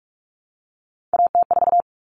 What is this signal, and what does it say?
Morse code, heard as a keyed tone, about 44 wpm, 700 Hz. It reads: AT4
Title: Common Cut Numbers - 3-Digit 45wpm